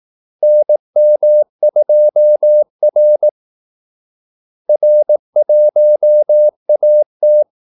Morse code, heard as a keyed tone, about 18 wpm, 600 Hz. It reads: NM2R R1AT
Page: Kaldesignaler 18 wpm